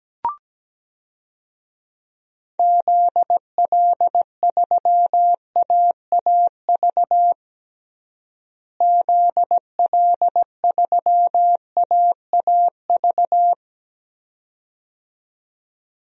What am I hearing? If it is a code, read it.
Morse code, heard as a keyed tone, about 17 wpm, 700 Hz. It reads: ZL3AAV ZL3AAV
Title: DXCC-100 Call Signs - 6 Characters - Repeated 2x - 17wpm